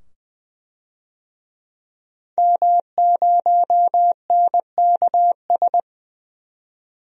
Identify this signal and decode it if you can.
Morse code, heard as a keyed tone, about 20 wpm, 700 Hz. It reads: M0NKS
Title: DXCC-100 Call Signs - 5 Characters - 20wpm